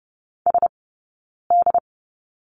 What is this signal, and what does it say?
Morse code, heard as a keyed tone, about 30 wpm, spaced as for 9 wpm, 690 Hz. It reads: SD
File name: Lesson 19d - Letter w - Two Characters 30wpm